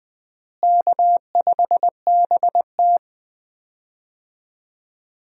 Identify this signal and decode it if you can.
Morse code, heard as a keyed tone, about 20 wpm, 700 Hz. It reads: K5BT
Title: USA Call Signs - 4 Characters - Repeated 3x - 20wpm